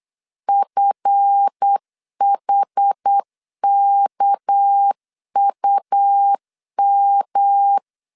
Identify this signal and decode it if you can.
Morse code, heard as a keyed tone, about 8 wpm, 800 Hz. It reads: FHKUM
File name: LEZIONI / 8 YJ45 / 8 YJ45 8wpm 3214